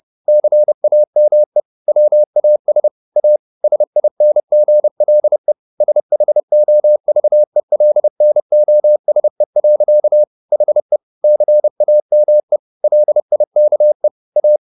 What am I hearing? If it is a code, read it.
Morse code, heard as a keyed tone, about 30 wpm, 600 Hz. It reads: CAME WAS A SINGLE SHOVELNOSE. HE CAME LIKE A